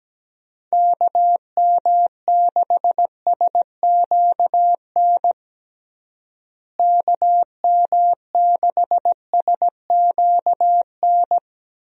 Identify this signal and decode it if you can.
Morse code, heard as a keyed tone, about 17 wpm, 700 Hz. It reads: KM6SQN KM6SQN
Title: USA Call Signs - 6 Characters - Repeated 2x - 17wpm